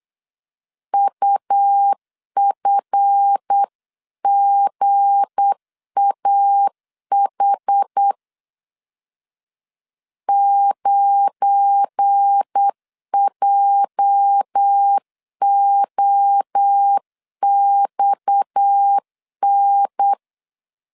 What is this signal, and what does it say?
Morse code, heard as a keyed tone, about 8 wpm, 800 Hz. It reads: UFGAH 9JOXN
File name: LEZIONI / 9 2 8wpm 3437